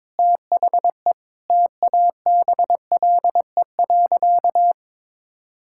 Morse code, heard as a keyed tone, about 21 wpm, 700 Hz. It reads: THE TABLE.
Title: Sentences from Top 1600 Words 22wpm